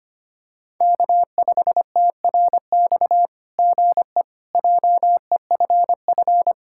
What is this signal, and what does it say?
Morse code, heard as a keyed tone, about 25 wpm, 700 Hz. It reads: K5TRX GE JEFF
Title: SOTA QSO Practice 25wpm